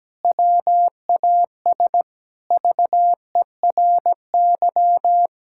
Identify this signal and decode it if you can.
Morse code, heard as a keyed tone, about 17 wpm, 700 Hz. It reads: WAS VERY
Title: Sentences from Top 1000 Words - Review Part 6 - 17wpm